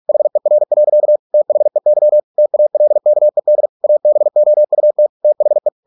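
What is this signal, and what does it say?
Morse code, heard as a keyed tone, about 46 wpm, 600 Hz. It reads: HER. THEY TALKED ABOUT THE
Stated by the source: Morse practice at 45 wpm, the novel Never by Ken Follett in English, from CW med Gnister.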